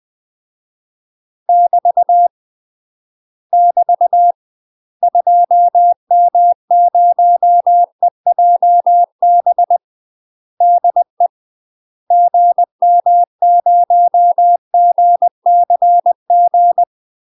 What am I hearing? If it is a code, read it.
Morse code, heard as a keyed tone, about 20 wpm, 700 Hz. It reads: <BT> <BT> 2M0EJB DE GM0GCG
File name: UK MASTER QSOs TRAINING / 20WPM / UK MASTER QSOs 0055